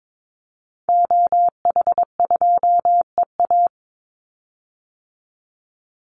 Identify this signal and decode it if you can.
Morse code, heard as a keyed tone, about 22 wpm, 700 Hz. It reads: OH2EA
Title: Call Signs Repeated 3x - 22wpm